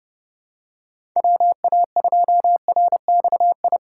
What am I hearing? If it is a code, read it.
Morse code, heard as a keyed tone, about 30 wpm, 700 Hz. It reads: WA2RXI
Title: USA Call Signs - 6 Characters - 2x Character Spacing - 30wpm